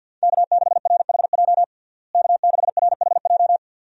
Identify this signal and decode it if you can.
Morse code, heard as a keyed tone, about 47 wpm, 700 Hz. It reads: K6RHJ K6RHJ
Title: USA Call Signs - 5 Characters - Repeated 2x - 50wpm